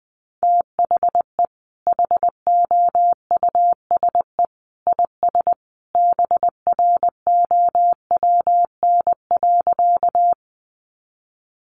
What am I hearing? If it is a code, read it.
Morse code, heard as a keyed tone, about 20 wpm, 700 Hz. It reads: THE HOUSE IS BROWN.
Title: Sentences from Top 1200 Words 20wpm